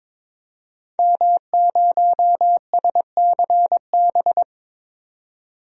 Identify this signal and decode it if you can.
Morse code, heard as a keyed tone, about 22 wpm, 700 Hz. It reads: M0SCB
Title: DXCC-100 Call Signs - 5 Characters - 22wpm